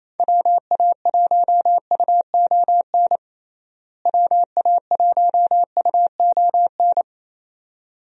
Morse code, heard as a keyed tone, about 28 wpm, 700 Hz. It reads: WA1UON WA1UON
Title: USA Call Signs - 6 Characters - Repeated 2x - 28wpm